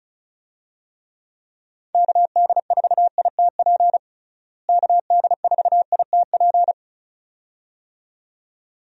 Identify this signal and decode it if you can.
Morse code, heard as a keyed tone, about 36 wpm, 700 Hz. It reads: KD4ITP KD4ITP
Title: USA Call Signs - 6 Characters - Repeated 2x - 35wpm